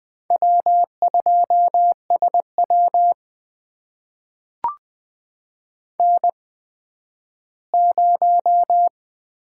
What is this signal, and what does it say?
Morse code, heard as a keyed tone, about 20 wpm, spaced as for 12 wpm, 700 Hz. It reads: W2SW N0
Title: USA Call Signs - 4 Characters - 8x Character Spacing - 20wpm